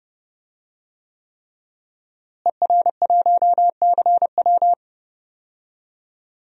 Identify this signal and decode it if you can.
Morse code, heard as a keyed tone, about 30 wpm, 700 Hz. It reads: ER1CW
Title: DXCC-100 Call Signs - 5 Characters - Repeated 3x - 30wpm